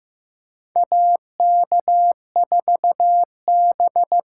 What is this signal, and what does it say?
Morse code, heard as a keyed tone, about 15 wpm, 700 Hz. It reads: AK4B
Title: USA Call Signs - 5 Characters - 6x Character Spacing - 15wpm